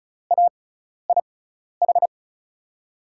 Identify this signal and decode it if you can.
Morse code, heard as a keyed tone, about 36 wpm, spaced as for 12 wpm, 700 Hz. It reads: AIH
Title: USA Call Signs - 6 Characters - 6x Character Spacing - 35wpm